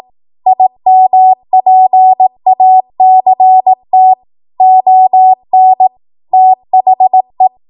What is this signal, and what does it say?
Morse code, heard as a keyed tone, about 18 wpm, 750 Hz. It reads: IMPACT ON THE
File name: Morse / 18WPM